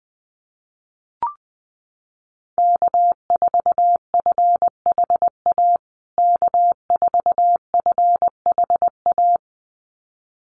Morse code, heard as a keyed tone, about 20 wpm, 700 Hz. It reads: K4FHA K4FHA
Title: USA Call Signs Repeated 2x 20wpm